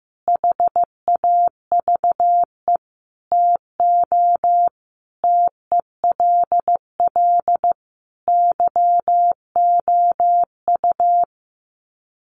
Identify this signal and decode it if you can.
Morse code, heard as a keyed tone, about 15 wpm, 700 Hz. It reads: HAVE TO TELL YOU
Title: Sets of 5 Words 15wpm